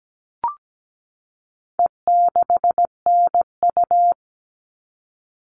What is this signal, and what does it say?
Morse code, heard as a keyed tone, about 17 wpm, 700 Hz. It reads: E6NU
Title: All Cut Numbers - 4-Digit 17wpm